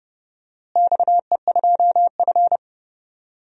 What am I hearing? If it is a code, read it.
Morse code, heard as a keyed tone, about 29 wpm, 700 Hz. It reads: XE2F
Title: DXCC-100 Call Signs - 4 Characters - Repeated 3x - 30wpm